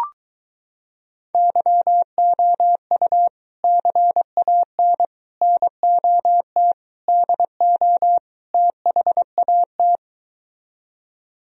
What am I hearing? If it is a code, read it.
Morse code, heard as a keyed tone, about 23 wpm, 700 Hz. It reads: YOU CAN NOT DO THAT